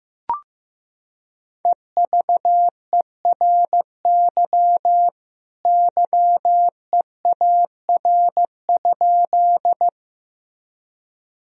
Morse code, heard as a keyed tone, about 15 wpm, 700 Hz. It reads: EVERY YEAR?
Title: Sentences from Top 1000 Words - Review Part 1 - 15wpm